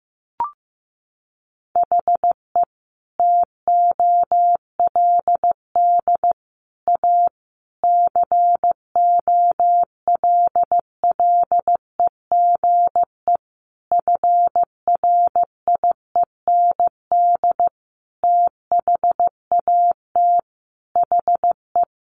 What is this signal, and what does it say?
Morse code, heard as a keyed tone, about 15 wpm, 700 Hz. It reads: HE TOLD A COLLEGE FRIEND THAT HE
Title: Sentences from Top 1300 Words 15wpm